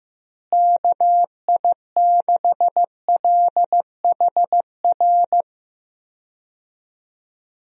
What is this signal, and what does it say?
Morse code, heard as a keyed tone, about 15 wpm, 700 Hz. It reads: KI6LHR
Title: USA Call Signs - 6 Characters - Repeated 3x - 15wpm